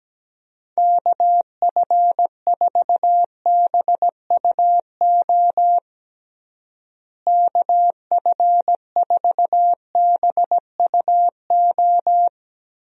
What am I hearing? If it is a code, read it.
Morse code, heard as a keyed tone, about 17 wpm, 700 Hz. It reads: KF4BUO KF4BUO
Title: USA Call Signs - 6 Characters - Repeated 3x - 17wpm